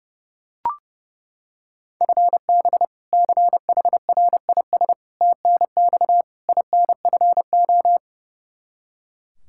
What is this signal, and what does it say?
Morse code, heard as a keyed tone, about 30 wpm, 700 Hz. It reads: FB CHRIS TNX INFO
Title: Practice QSO - Exchange 3 - More Info - 30wpm